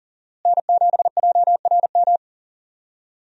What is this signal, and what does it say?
Morse code, heard as a keyed tone, about 40 wpm, 700 Hz. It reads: N7JRM
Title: USA Call Signs - 5 Characters - Repeated 2x - 40wpm